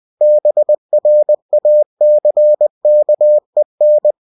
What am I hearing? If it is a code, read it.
Morse code, heard as a keyed tone, about 20 wpm, 600 Hz. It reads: BRACKEN
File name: cw-books / winnie-the-pooh / winnie-20wpm-page-0044